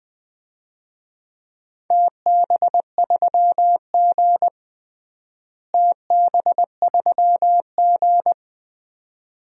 Morse code, heard as a keyed tone, about 20 wpm, 700 Hz. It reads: TB3G TB3G
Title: DXCC-100 Call Signs - 4 Characters - Repeated 2x - 20wpm